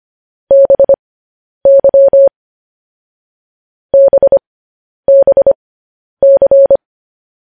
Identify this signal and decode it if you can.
Morse code, heard as a keyed tone, about 25 wpm, spaced as for 10 wpm, 550 Hz. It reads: BY BBC
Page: news-headlines-in-morse-code-at-10-wpm